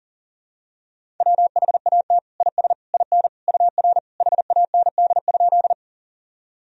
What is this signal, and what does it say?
Morse code, heard as a keyed tone, about 40 wpm, 700 Hz. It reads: WHAT IS IN UR HAND?
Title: Sentences from Top 200 Words and Abbreviations 40wpm